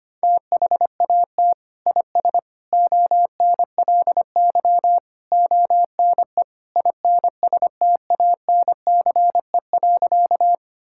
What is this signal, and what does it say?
Morse code, heard as a keyed tone, about 25 wpm, 700 Hz. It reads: THAT IS ONLY ONE INSTANCE.
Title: Sentences from Top 1000 Words - Review Part 5 - 25wpm